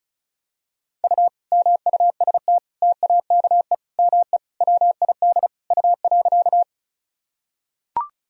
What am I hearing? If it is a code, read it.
Morse code, heard as a keyed tone, about 36 wpm, 700 Hz. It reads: U MUST TAKE ME WID U.
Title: Sentences from Top 200 Words and Abbreviations 35wpm